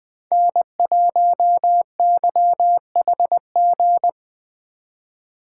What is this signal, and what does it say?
Morse code, heard as a keyed tone, about 20 wpm, 700 Hz. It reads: N1YHG
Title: USA Call Signs - 3x Character Spacing - 20wpm